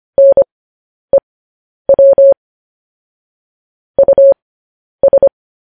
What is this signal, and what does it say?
Morse code, heard as a keyed tone, about 25 wpm, spaced as for 10 wpm, 550 Hz. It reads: NEW US
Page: News Headlines in Morse Code at 10 WPM